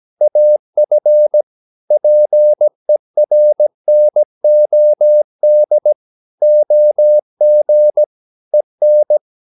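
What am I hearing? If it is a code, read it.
Morse code, heard as a keyed tone, about 17 wpm, 600 Hz. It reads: AF PERNOD OG EN
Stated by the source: Slow (QRS) Morse practice at 17 wpm, from CW med Gnister.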